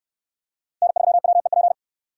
Morse code, heard as a keyed tone, about 61 wpm, 700 Hz. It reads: N3WEP